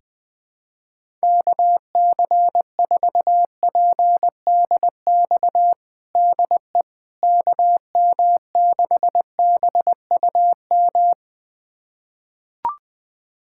Morse code, heard as a keyed tone, about 20 wpm, 700 Hz. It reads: KC4PDX DE KM6BUM
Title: USA Callsign DE Callsign 20wpm